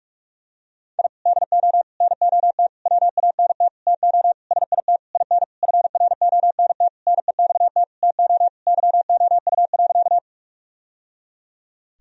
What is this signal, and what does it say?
Morse code, heard as a keyed tone, about 46 wpm, 700 Hz. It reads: I DO NOT WANT TO SIT IN FRONT NEXT TO YOU.
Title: Sentences from Top 1000 Words - Review Part 2 - 45wpm